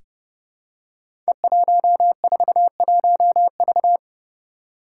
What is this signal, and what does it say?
Morse code, heard as a keyed tone, about 30 wpm, 700 Hz. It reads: E141V
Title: Five Challenging Letter-Number Combinations 30wpm